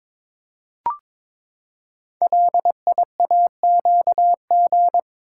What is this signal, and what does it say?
Morse code, heard as a keyed tone, about 22 wpm, 700 Hz. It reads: LIAQG
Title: Code Groups - 5 Letters - 22wpm